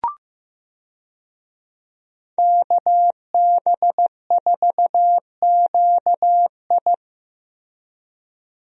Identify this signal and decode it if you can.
Morse code, heard as a keyed tone, about 15 wpm, 700 Hz. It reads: KB4QI
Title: USA Call Signs - 5 Characters - Repeated 3x - 15wpm